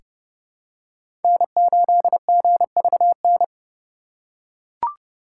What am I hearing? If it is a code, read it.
Morse code, heard as a keyed tone, about 32 wpm, 700 Hz. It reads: N8GVN
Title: USA Call Signs - 5 Characters - 4x Character Spacing - 30wpm